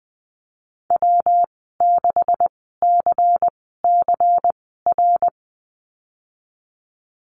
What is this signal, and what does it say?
Morse code, heard as a keyed tone, about 20 wpm, spaced as for 15 wpm, 700 Hz. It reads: W6CCR
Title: 147 Lesson 24g - Letter y - Call Signs 20wpm